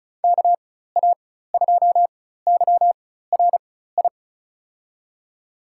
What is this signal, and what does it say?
Morse code, heard as a keyed tone, about 34 wpm, spaced as for 16 wpm, 690 Hz. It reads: KA2YRI
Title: USA Call Signs - 4x Character Spacing - 35wpm